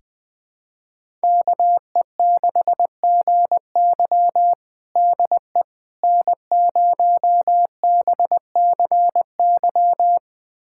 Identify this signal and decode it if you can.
Morse code, heard as a keyed tone, about 21 wpm, 700 Hz. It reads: KE6GY DE N0BCY
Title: USA Callsign DE Callsign 20wpm